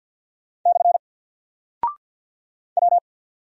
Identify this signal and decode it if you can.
Morse code, heard as a keyed tone, about 48 wpm, 700 Hz. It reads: / W
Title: Single Character (Letter:Number:Punctuation:Prosign) 50wpm